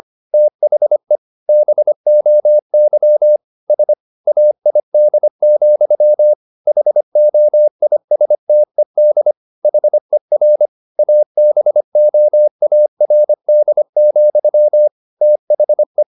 Morse code, heard as a keyed tone, about 25 wpm, 600 Hz. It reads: THE BOY SAID, HOISTED HER ABOARD, THE
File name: the old man and the sea part 1 25wpm